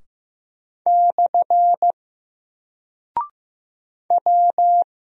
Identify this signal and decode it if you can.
Morse code, heard as a keyed tone, about 15 wpm, 700 Hz. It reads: / W
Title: Single Character (Letter:Number:Punctuation:Prosign) 15wpm